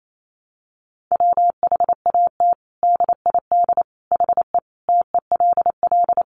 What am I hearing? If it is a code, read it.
Morse code, heard as a keyed tone, about 28 wpm, 700 Hz. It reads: WHAT DID HE TELL
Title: Sentences from Top 200 Words 28wpm